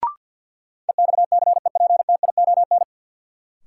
Morse code, heard as a keyed tone, about 47 wpm, 700 Hz. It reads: EXCEPTION
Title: Sets of 4 Words - New Word Drill - 50wpm